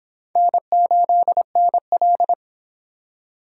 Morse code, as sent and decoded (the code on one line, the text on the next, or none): -. ---.. -. .-..
N8NL